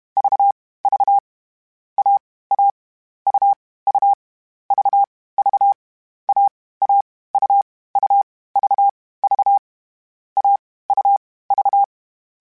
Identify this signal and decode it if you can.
Morse code, heard as a keyed tone, about 32 wpm, 800 Hz. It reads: V V A A U U V V A A U U V V A U V
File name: Difficult Combos / Difficult Combos-mp3s / AUV AUV4 @30